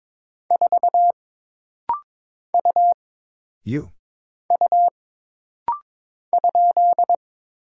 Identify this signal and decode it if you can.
Morse code, heard as a keyed tone, about 22 wpm, 700 Hz. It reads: 4 U U ?
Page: This rapid-fire timing is geared to push your ICR (Instant Character Recognition) skills.